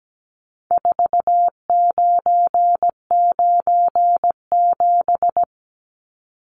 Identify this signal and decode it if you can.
Morse code, heard as a keyed tone, about 17 wpm, 700 Hz. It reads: 4997
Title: Numbers - 4-Digit 17wpm